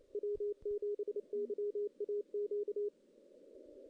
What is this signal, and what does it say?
Morse code, heard as a keyed tone, about 28 wpm, 410 Hz. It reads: W7YAQ
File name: 2016 JAN CW NAQP / 2016 Jan naqp k5ot - 0-5